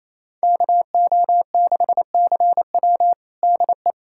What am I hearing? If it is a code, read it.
Morse code, heard as a keyed tone, about 28 wpm, 700 Hz. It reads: KO6CW DE
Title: USA Callsign DE Callsign - 5 Characters - 28wpm